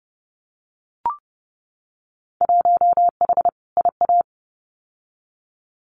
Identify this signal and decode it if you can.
Morse code, heard as a keyed tone, about 31 wpm, 700 Hz. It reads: 1H IA
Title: Winter Field Day Exchanges 30wpm